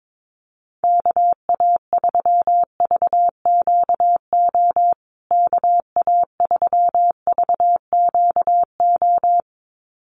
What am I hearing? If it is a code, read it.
Morse code, heard as a keyed tone, about 22 wpm, 700 Hz. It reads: KA3VQO KA3VQO
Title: USA Call Signs Repeated 2x 22wpm